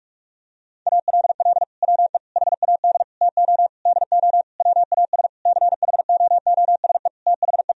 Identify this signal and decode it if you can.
Morse code, heard as a keyed tone, about 46 wpm, 700 Hz. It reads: ALL WE HAD TO DO WAS CHOOSE THE
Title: Sentences from Top 1000 Words - Review Part 4 - 45wpm